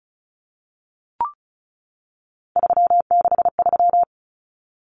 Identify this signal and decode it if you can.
Morse code, heard as a keyed tone, about 34 wpm, 700 Hz. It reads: 363
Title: Common Cut Numbers - 3-Digit 35wpm